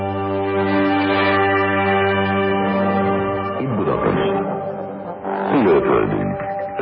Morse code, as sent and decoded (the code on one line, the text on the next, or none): none